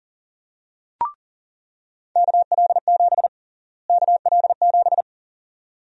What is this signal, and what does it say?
Morse code, heard as a keyed tone, about 37 wpm, 700 Hz. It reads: KL7 KL7
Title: USA Call Sign Prefixes - Repeated 2x - 40wpm